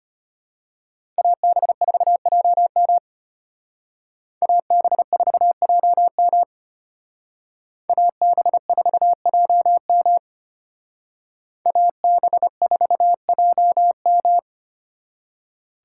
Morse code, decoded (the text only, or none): AB4JM AB4JM AB4JM AB4JM